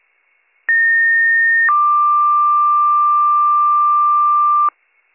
Two-tone page, heard as a one-second tone, then a three-second tone, about 1850 then 1180 Hz. It is heard from 0.7 to 4.7 s.